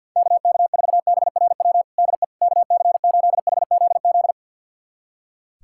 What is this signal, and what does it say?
Morse code, heard as a keyed tone, about 45 wpm, 700 Hz. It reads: KK4BRW DE KC8HZZ
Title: USA Callsign DE Callsign - 6 Characters - 50wpm